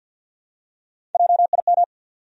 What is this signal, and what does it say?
Morse code, heard as a keyed tone, about 53 wpm, 700 Hz. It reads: JIM
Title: USA Common Names - Max 3 Letters - 50wpm